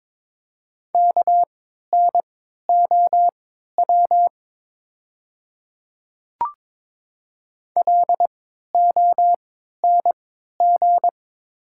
Morse code, heard as a keyed tone, about 23 wpm, spaced as for 13 wpm, 700 Hz. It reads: KNOW LONG
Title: 3x Character Spacing - Top 100 Words - 22wpm